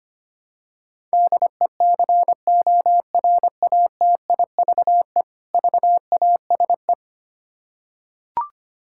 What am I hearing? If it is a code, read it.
Morse code, heard as a keyed tone, about 25 wpm, 700 Hz. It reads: DECORATIVE VASE
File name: practice / 25wpm / Difficult Words with Hints - 25wpm